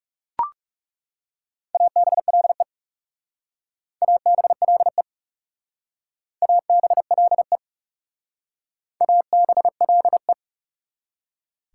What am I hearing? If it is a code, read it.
Morse code, heard as a keyed tone, about 36 wpm, 690 Hz. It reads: ABLE ABLE ABLE ABLE
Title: Speed-Racing Prefixes and Suffixes 30wpm